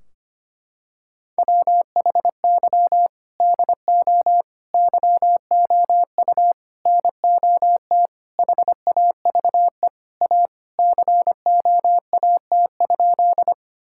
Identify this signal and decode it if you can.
Morse code, heard as a keyed tone, about 25 wpm, 700 Hz. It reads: WHY DO YOU NOT HAVE A COAT?
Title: Sentences from Top 1600 Words 25wpm